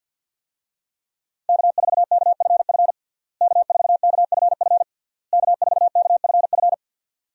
Morse code, heard as a keyed tone, about 51 wpm, 700 Hz. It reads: K4KFF K4KFF K4KFF
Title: USA Call Signs - 5 Characters - Repeated 3x - 50wpm